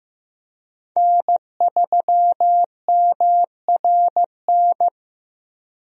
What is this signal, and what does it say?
Morse code, heard as a keyed tone, about 15 wpm, 700 Hz. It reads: N3MRN
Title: USA Call Signs - 5 Characters - Repeated 3x - 15wpm